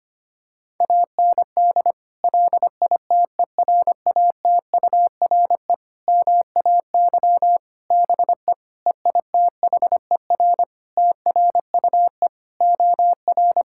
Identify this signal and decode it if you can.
Morse code, heard as a keyed tone, about 25 wpm, 700 Hz. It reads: AND LITERATURE MAY BE EITHER TRUE OR
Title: Sentences from Top 1600 Words 25wpm